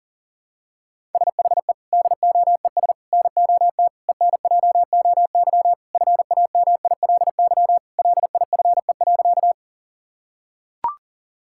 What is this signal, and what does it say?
Morse code, heard as a keyed tone, about 40 wpm, 700 Hz. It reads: SHE DOES NOT ENJOY FAMILY LIFE.
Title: Sentences from Top 1000 Words - Review Part 5 - 40wpm